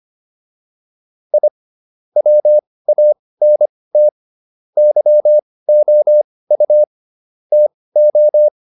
Morse code, heard as a keyed tone, about 25 wpm, spaced as for 18 wpm, 600 Hz. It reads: I WANT YOU TO